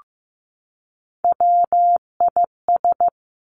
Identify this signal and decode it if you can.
Morse code, heard as a keyed tone, about 15 wpm, 700 Hz. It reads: WIS
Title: Three Challenging Letter-Number Combinations 15wpm